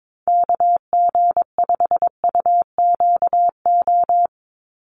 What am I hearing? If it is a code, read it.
Morse code, heard as a keyed tone, about 22 wpm, 700 Hz. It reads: KG5UQO
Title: USA Call Signs 22wpm